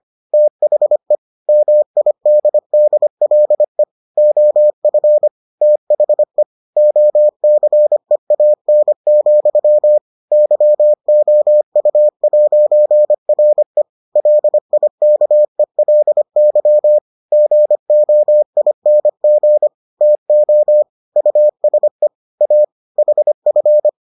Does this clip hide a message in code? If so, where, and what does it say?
Yes: Morse code from 0.3 to 23.9 s, heard as a keyed tone, about 25 wpm, 600 Hz. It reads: THE MIDDLE OF THE OCEAN, YOU'RE LIKELY GOING TO USE A HF